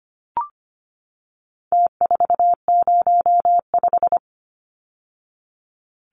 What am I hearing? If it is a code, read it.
Morse code, heard as a keyed tone, about 25 wpm, 700 Hz. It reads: T405
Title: Numbers - Contest Style - 4-Digit 25wpm